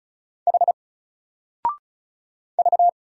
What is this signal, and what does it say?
Morse code, heard as a keyed tone, about 36 wpm, 700 Hz. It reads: H V
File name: Single-Letter-Number-35wpm